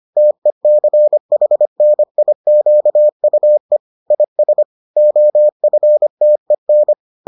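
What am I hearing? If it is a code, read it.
Morse code, heard as a keyed tone, about 25 wpm, 600 Hz. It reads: TECHNIQUE IS OFTEN
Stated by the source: Morse edition of the Foundations of Amateur Radio episodes.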